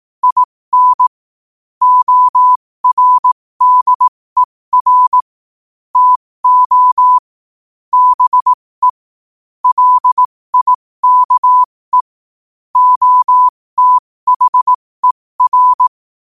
Morse code, heard as a keyed tone, about 18 wpm, spaced as for 15 wpm, 1000 Hz. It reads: IN ORDER TO BE LIKE OTHER
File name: qotdfiles / 15wpm